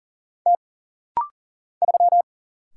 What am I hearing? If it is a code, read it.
Morse code, heard as a keyed tone, about 40 wpm, 700 Hz. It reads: T 3
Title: Single Letter-Number - Rapid-Fire 40wpm